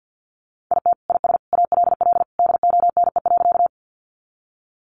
Morse code, heard as a keyed tone, about 50 wpm, 700 Hz. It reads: IT IS ALL DONE.
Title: Sentences from Top 500 Words 50wpm